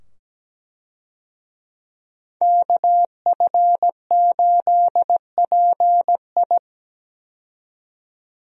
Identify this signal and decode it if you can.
Morse code, heard as a keyed tone, about 17 wpm, 700 Hz. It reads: KF8PI
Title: USA Call Signs - 5 Characters - 17wpm